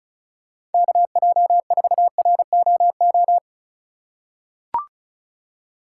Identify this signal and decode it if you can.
Morse code, heard as a keyed tone, about 36 wpm, 700 Hz. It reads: KJ4ROO
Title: USA Call Signs - 6 Characters - 2x Character Spacing - 35wpm